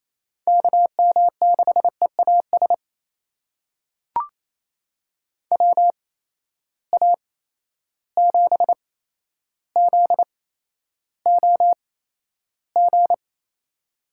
Morse code, heard as a keyed tone, about 28 wpm, spaced as for 12 wpm, 700 Hz. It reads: KM6EAS WA7ZOG
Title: USA Call Signs - 6 Characters - 8x Character Spacing - 28wpm